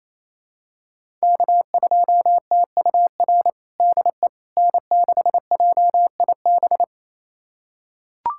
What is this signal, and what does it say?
Morse code, heard as a keyed tone, about 27 wpm, 700 Hz. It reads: K2TUR DE N6JIB
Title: USA Callsign DE Callsign - 5 Characters - 28wpm